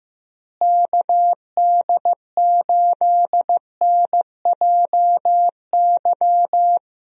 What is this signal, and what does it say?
Morse code, heard as a keyed tone, about 15 wpm, 700 Hz. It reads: KD8NJY
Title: USA Call Signs - 6 Characters - Repeated 2x - 15wpm